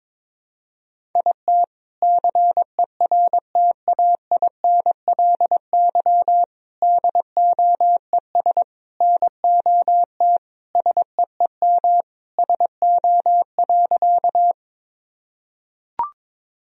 Morse code, heard as a keyed tone, about 22 wpm, 700 Hz. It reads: IT CERTAINLY DOES NOT SEEM SO.
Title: Sentences from Top 1000 Words - Review Part 1 - 22wpm